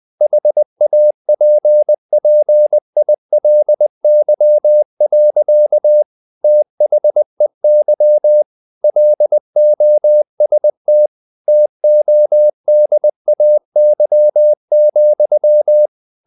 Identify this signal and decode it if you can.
Morse code, heard as a keyed tone, about 20 wpm, 600 Hz. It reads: HAPPILY. THEY LOST TODAY,
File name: the old man and the sea / the old man and the sea part 1 20wpm